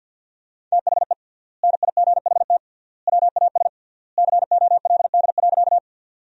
Morse code, heard as a keyed tone, about 47 wpm, 700 Hz. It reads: THE NIGHT WAS COLD.
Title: 3x Word-Spacing - Sentences from Top 400 Words - 50wpm